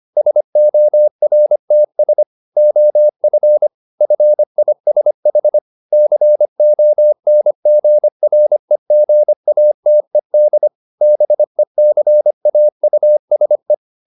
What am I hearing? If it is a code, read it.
Morse code, heard as a keyed tone, about 25 wpm, 600 Hz. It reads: SORTS OF FISH CONGREGATED BECAUSE
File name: the old man and the sea part 1 25wpm